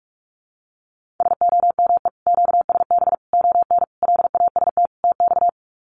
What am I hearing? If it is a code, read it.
Morse code, heard as a keyed tone, about 44 wpm, 700 Hz. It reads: SOME QSB ON LAST TX
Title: Common QSO Phrases 45wpm